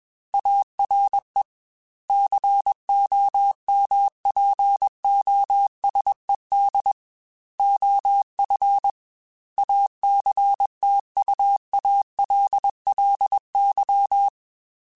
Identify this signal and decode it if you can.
Morse code, heard as a keyed tone, about 21 wpm, 800 Hz. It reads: ARE COMPOSED OF ACTUALLY
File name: Technical Subjects / Technical Subjects-mp3s / Sw Pwr Sup @20